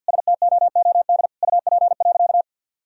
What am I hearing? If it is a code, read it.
Morse code, heard as a keyed tone, about 50 wpm, 700 Hz. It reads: STOOD UP.
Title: Sentences from Top 1000 Words - Review Part 2 - 50wpm